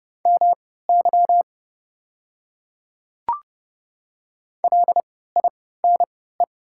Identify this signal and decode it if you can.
Morse code, heard as a keyed tone, about 30 wpm, spaced as for 17 wpm, 690 Hz. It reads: MY LINE
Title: 3x Character Spacing - Top 200 Words - 30wpm